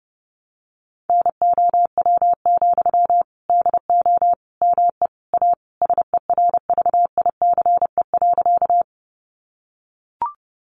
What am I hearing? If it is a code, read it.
Morse code, heard as a keyed tone, about 30 wpm, 700 Hz. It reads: NOW, DO ME A SERVICE.
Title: Sentences from Top 700 Words 30wpm